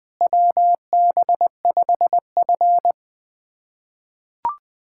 Morse code, heard as a keyed tone, about 20 wpm, 700 Hz. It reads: WB5F
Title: USA Callsign DE Callsign - 4 Characters - 20wpm